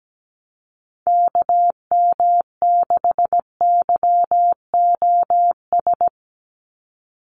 Morse code, heard as a keyed tone, about 17 wpm, 700 Hz. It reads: KM6YOS
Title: USA Call Signs 17wpm